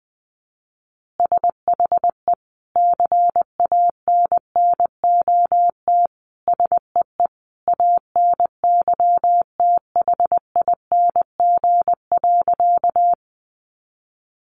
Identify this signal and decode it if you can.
Morse code, heard as a keyed tone, about 20 wpm, 700 Hz. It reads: SHE CANNOT SEE ANYTHING.
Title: Additional Easy Phrases 20wpm